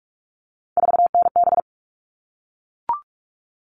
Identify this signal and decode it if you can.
Morse code, heard as a keyed tone, about 44 wpm, 700 Hz. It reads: 4NB
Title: All Cut Numbers - 3-Digit 45wpm